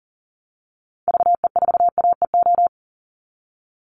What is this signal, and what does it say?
Morse code, heard as a keyed tone, about 40 wpm, 700 Hz. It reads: VE4AEO
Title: Call Signs Repeated 2x - 40wpm